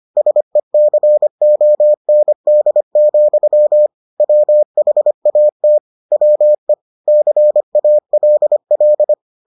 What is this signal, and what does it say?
Morse code, heard as a keyed tone, about 25 wpm, 600 Hz. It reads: SECOND, WHAT WE CALL